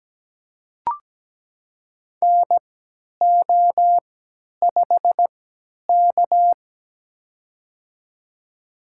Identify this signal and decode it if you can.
Morse code, heard as a keyed tone, about 17 wpm, spaced as for 10 wpm, 700 Hz. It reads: NO5K
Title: USA Call Signs - 4 Characters - 3x Character Spacing - 17wpm